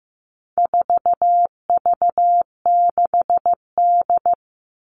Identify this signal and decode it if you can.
Morse code, heard as a keyed tone, about 15 wpm, 700 Hz. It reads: 4V6D
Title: All Cut Numbers - 4-Digit 15wpm